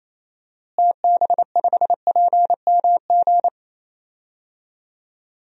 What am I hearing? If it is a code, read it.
Morse code, heard as a keyed tone, about 28 wpm, 700 Hz. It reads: TB5PMG
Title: DXCC-100 Call Signs - 6 Characters - Repeated 3x - 28wpm